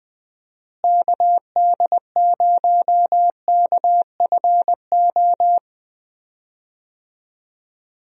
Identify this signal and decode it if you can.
Morse code, heard as a keyed tone, about 20 wpm, 700 Hz. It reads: KD0KFO
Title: USA Call Signs - 6 Characters - Repeated 3x - 20wpm